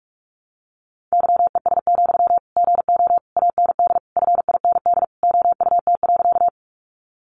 Morse code, heard as a keyed tone, about 44 wpm, 700 Hz. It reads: YES, GO AND FIND OUT.
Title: Sentences from Top 500 Words 45wpm